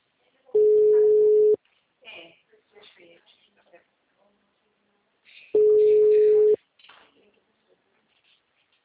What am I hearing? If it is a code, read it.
Ringback tone (Europe, 425 Hz), with speech.